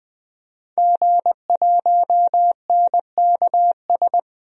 Morse code, heard as a keyed tone, about 21 wpm, 700 Hz. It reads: G1NKS
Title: DXCC-100 Call Signs - 5 Characters - Repeated 3x - 20wpm